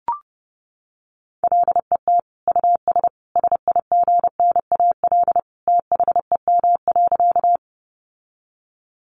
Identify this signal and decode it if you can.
Morse code, heard as a keyed tone, about 30 wpm, 700 Hz. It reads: LET US SIGNAL THEM.
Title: Sentences from Top 1400 Words 30wpm